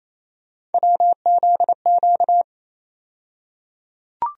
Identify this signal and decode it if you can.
Morse code, heard as a keyed tone, about 28 wpm, 700 Hz. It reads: WZQ
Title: USA Call Sign Postfixes - 3 Letters 28wpm